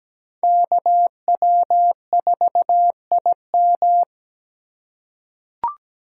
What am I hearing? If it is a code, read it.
Morse code, heard as a keyed tone, about 17 wpm, 700 Hz. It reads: KW4IM
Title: USA Call Signs - 5 Characters - 8x Character Spacing - 17wpm